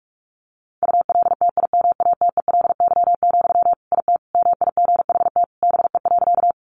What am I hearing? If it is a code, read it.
Morse code, heard as a keyed tone, about 44 wpm, 700 Hz. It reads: ULTIMATELY, IT MIGHT BE.
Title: Sentences from Top 1200 Words 45wpm